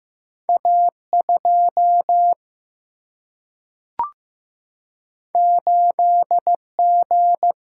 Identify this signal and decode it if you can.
Morse code, heard as a keyed tone, about 15 wpm, 700 Hz. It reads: A 2 8 G
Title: Single-Letter Single-Number 15wpm